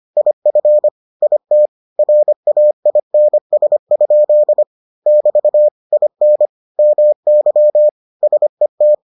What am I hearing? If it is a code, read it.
Morse code, heard as a keyed tone, about 25 wpm, 600 Hz. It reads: IF IT RAINS? <BT> IN MY SET